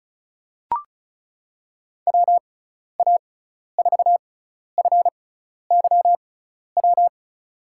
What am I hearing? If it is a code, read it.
Morse code, heard as a keyed tone, about 36 wpm, spaced as for 12 wpm, 690 Hz. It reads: WA4FYW
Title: USA Call Signs - 6 Characters - 6x Character Spacing - 35wpm